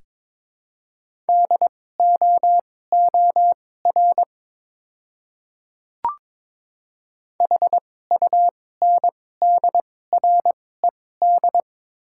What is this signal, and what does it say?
Morse code, heard as a keyed tone, about 22 wpm, spaced as for 16 wpm, 700 Hz. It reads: DOOR HUNDRED
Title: 2x Character Spacing - Top 500 Words - 22wpm